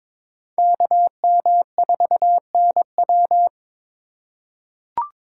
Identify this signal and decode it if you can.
Morse code, heard as a keyed tone, about 22 wpm, 700 Hz. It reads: KM4NW
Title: USA Call Signs - 5 Characters - 2x Character Spacing - 22wpm